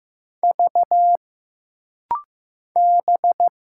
Morse code, heard as a keyed tone, about 15 wpm, 700 Hz. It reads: V B
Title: Single Letters - Rapid-Fire 15wpm